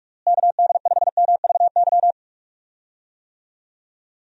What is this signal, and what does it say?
Morse code, heard as a keyed tone, about 46 wpm, 700 Hz. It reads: KD5MVY
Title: USA Call Signs - 6 Characters - Repeated 2x - 45wpm